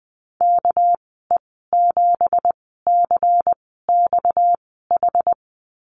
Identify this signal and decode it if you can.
Morse code, heard as a keyed tone, about 20 wpm, spaced as for 14 wpm, 700 Hz. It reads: KE7CXH
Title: 238 Lesson 38g - Letter q - Call Signs 20wpm